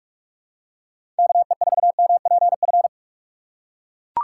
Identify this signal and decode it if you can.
Morse code, heard as a keyed tone, about 44 wpm, 700 Hz. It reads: KE4MPF
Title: USA Call Signs - 6x Character Spacing - 45wpm